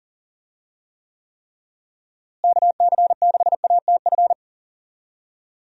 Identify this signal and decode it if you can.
Morse code, heard as a keyed tone, about 40 wpm, 700 Hz. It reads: KC6ATF
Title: USA Call Signs - 6 Characters - Repeated 3x - 40wpm